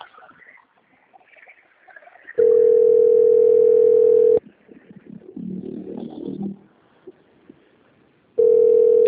Ringback tone (North America), 440 + 480 Hz.